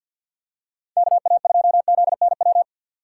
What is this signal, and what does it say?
Morse code, heard as a keyed tone, about 51 wpm, 700 Hz. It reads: KA2ZNW
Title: USA Call Signs - 6 Characters - 6x Character Spacing - 50wpm